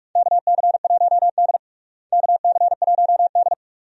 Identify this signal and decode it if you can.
Morse code, heard as a keyed tone, about 46 wpm, 700 Hz. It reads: KC1D KC1D
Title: USA Call Signs - 4 Characters - Repeated 2x - 45wpm